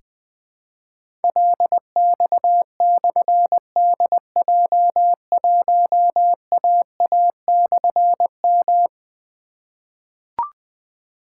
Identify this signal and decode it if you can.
Morse code, heard as a keyed tone, about 21 wpm, 700 Hz. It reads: LX/DJ1AA/M
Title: Difficult Call Signs 20wpm